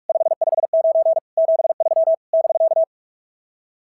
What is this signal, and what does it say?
Morse code, heard as a keyed tone, about 42 wpm, 660 Hz. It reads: 559 73 <BK>
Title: SOTA QSO Practice 45wpm